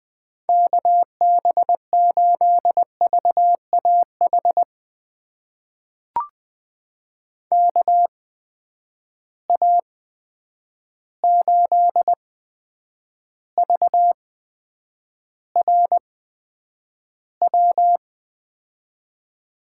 Morse code, heard as a keyed tone, about 20 wpm, spaced as for 9 wpm, 700 Hz. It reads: KB8VAH KA8VRW
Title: USA Call Signs - 6 Characters - 8x Character Spacing - 20wpm